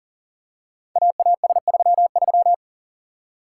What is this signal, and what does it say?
Morse code, heard as a keyed tone, about 38 wpm, 700 Hz. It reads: AAS33
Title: Code Groups - 5 Letter:Numbers - 40wpm